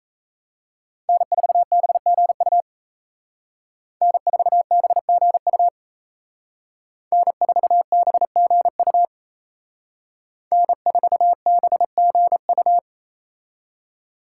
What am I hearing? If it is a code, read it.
Morse code, heard as a keyed tone, about 35 wpm, 700 Hz. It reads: N4BGU N4BGU N4BGU N4BGU